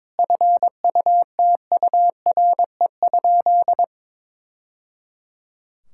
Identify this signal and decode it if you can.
Morse code, heard as a keyed tone, about 22 wpm, 700 Hz. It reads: FUTURE?
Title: Sentences from Top 1000 Words - Review Part 6 - 22wpm